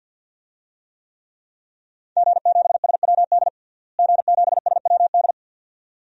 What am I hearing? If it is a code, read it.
Morse code, heard as a keyed tone, about 53 wpm, 700 Hz. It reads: G7SWD G7SWD
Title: DXCC-100 Call Signs - 5 Characters - Repeated 2x - 50wpm